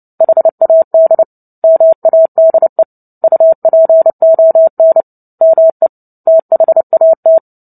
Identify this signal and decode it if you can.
Morse code, heard as a keyed tone, about 30 wpm, 650 Hz. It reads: HAD MADE UPON ME THAT
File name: Book Two / 03-Chapter 3 - 29 WPM